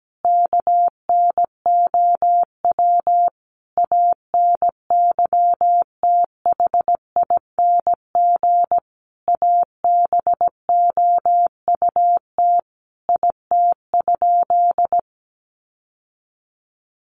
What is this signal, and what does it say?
Morse code, heard as a keyed tone, about 17 wpm, 700 Hz. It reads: KNOW ANYTHING ABOUT IT?
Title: Sentences from Top 1000 Words 17wpm